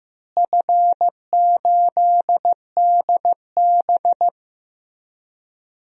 Morse code, heard as a keyed tone, about 15 wpm, 700 Hz. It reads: F8DB
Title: DXCC-100 Call Signs - 4 Characters - Repeated 3x - 15wpm